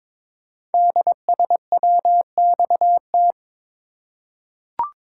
Morse code, heard as a keyed tone, about 22 wpm, 700 Hz. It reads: DSWXT
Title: Code Groups - 5 Letters - 22wpm